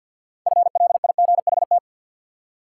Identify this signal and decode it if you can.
Morse code, heard as a keyed tone, about 52 wpm, 700 Hz. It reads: FLIGHT
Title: Six Letter Words - 50wpm